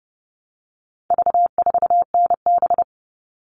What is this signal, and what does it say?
Morse code, heard as a keyed tone, about 30 wpm, 700 Hz. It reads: V4NB
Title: All Cut Numbers - 4-Digit 30wpm